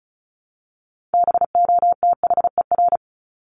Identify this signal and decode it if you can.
Morse code, heard as a keyed tone, about 34 wpm, 700 Hz. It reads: BOTHER
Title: Top 1501-1750 Words 35wpm